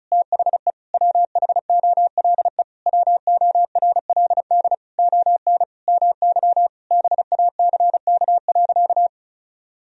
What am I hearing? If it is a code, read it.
Morse code, heard as a keyed tone, about 36 wpm, 700 Hz. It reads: THE WHOLE WORLD ON MY BACK.
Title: Sentences from Top 1000 Words - Review Part 1 - 35wpm